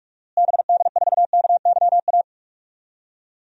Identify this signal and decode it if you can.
Morse code, heard as a keyed tone, about 46 wpm, 700 Hz. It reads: BD4KYA
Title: DXCC-100 Call Signs - 6 Characters - Repeated 2x - 45wpm